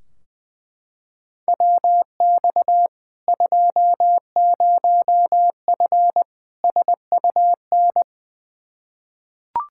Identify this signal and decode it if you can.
Morse code, heard as a keyed tone, about 20 wpm, 700 Hz. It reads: WX 20F SUN
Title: Practice QSO - Exchange 2 - Info - 20wpm